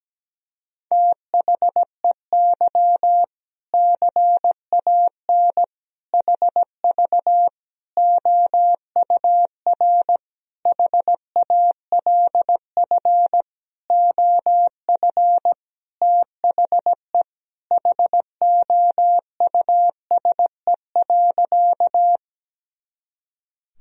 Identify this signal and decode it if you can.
Morse code, heard as a keyed tone, about 17 wpm, 700 Hz. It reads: THEY CAN HV OUR HALF OF THE HOUSE.
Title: Sentences from Top 400 Words and Abbreviations 17wpm